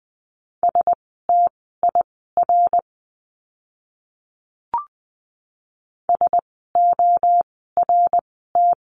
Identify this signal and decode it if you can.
Morse code, heard as a keyed tone, about 20 wpm, spaced as for 14 wpm, 700 Hz. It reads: STIR SORT
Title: 054 Lesson 10f - Letter r - Words 20wpm